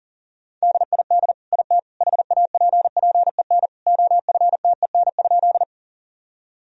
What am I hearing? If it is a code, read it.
Morse code, heard as a keyed tone, about 37 wpm, 700 Hz. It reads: DID IT HAPPEN OFTEN?
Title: Sentences from Top 1000 Words - Review Part 1 - 40wpm